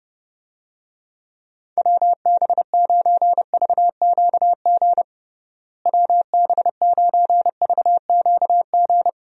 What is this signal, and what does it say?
Morse code, heard as a keyed tone, about 30 wpm, 700 Hz. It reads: WB9VQG WB9VQG
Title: USA Call Signs - 6 Characters - Repeated 2x - 30wpm